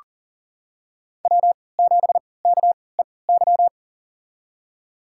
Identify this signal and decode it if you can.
Morse code, heard as a keyed tone, about 40 wpm, spaced as for 23 wpm, 700 Hz. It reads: W7KEY
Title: USA Call Signs - 5 Characters - 3x Character Spacing - 40wpm